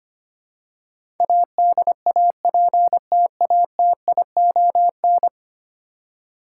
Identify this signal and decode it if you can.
Morse code, heard as a keyed tone, about 25 wpm, 700 Hz. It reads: ADAPTATION
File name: Ten Letter Words - 25wpm